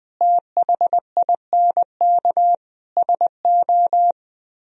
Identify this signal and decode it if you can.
Morse code, heard as a keyed tone, about 21 wpm, 700 Hz. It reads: THINK SO